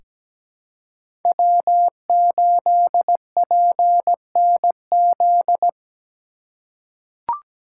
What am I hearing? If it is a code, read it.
Morse code, heard as a keyed tone, about 17 wpm, 700 Hz. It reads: W8PNZ
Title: USA Call Signs - 5 Characters - 3x Character Spacing - 17wpm